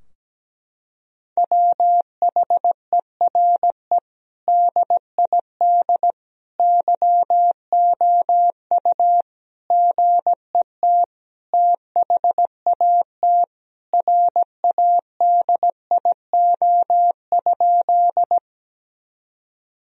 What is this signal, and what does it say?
Morse code, heard as a keyed tone, about 17 wpm, 700 Hz. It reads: WHERE DID YOU GET THAT RADIO?
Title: Sentences from Top 1000 Words - Review Part 3 - 17wpm